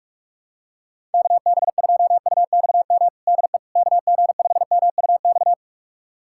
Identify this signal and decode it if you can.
Morse code, heard as a keyed tone, about 44 wpm, 700 Hz. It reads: KB2UXM DE KG5MUX
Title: USA Callsign DE Callsign - 6 Characters - 45wpm